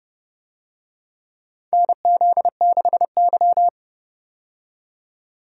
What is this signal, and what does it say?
Morse code, heard as a keyed tone, about 32 wpm, 700 Hz. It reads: NZ6Y
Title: USA Call Signs - 4 Characters - 30wpm